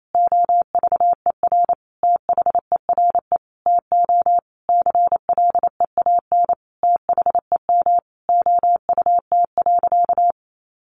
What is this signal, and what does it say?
Morse code, heard as a keyed tone, about 28 wpm, 700 Hz. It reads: OVER THERE TO CLEAN THEM OUT.
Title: Sentences from Top 1500 Words 28wpm